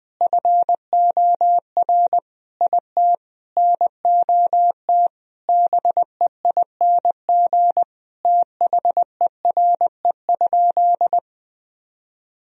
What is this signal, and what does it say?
Morse code, heard as a keyed tone, about 20 wpm, 700 Hz. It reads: FOR IT NOT BEING THERE?
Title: Sentences from Top 1000 Words - Review Part 3 - 20wpm